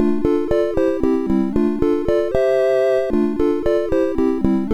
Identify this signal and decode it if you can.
There is no code.